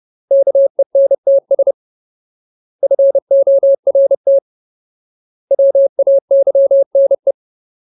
Morse code, heard as a keyed tone, about 30 wpm, 550 Hz. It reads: KENTS FORT WAYNE
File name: Ultimate Word List / 30WPM Ultimate